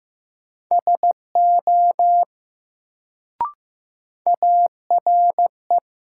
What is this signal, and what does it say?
Morse code, heard as a keyed tone, about 15 wpm, 700 Hz. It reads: SO ARE
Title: Sentences from Top 100 Words - New Word Drill - 15wpm